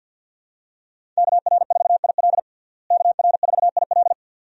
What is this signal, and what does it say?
Morse code, heard as a keyed tone, about 51 wpm, 700 Hz. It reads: KR4IL KR4IL
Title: USA Call Signs - 5 Characters - Repeated 2x - 50wpm